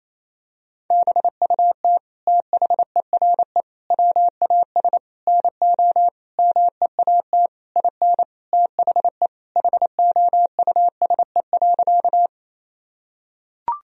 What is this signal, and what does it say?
Morse code, heard as a keyed tone, about 28 wpm, 700 Hz. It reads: BUT THERE WAS NO MEAT IN THE HOUSE.
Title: Sentences from Top 1000 Words - Review Part 3 - 28wpm